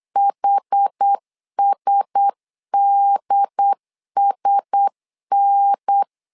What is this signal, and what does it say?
Morse code, heard as a keyed tone, about 8 wpm, 800 Hz. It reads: HSDSN
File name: LEZIONI / 4 Q1S 8wpm 2710